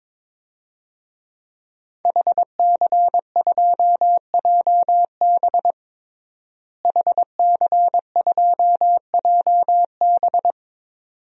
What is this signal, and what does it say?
Morse code, heard as a keyed tone, about 22 wpm, 700 Hz. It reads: HC2JB HC2JB
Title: DXCC-100 Call Signs - 5 Characters - Repeated 2x - 22wpm